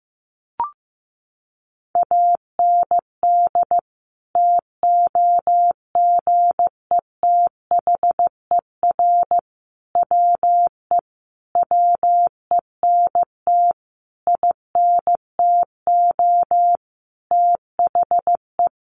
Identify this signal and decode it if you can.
Morse code, heard as a keyed tone, about 15 wpm, 700 Hz. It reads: AND TOGETHER WE WENT INTO THE
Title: Sentences from Top 600 Words 15wpm